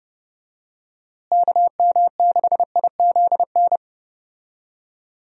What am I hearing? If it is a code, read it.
Morse code, heard as a keyed tone, about 30 wpm, 700 Hz. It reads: KM6IZN
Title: USA Call Signs - 6 Characters - Repeated 3x - 30wpm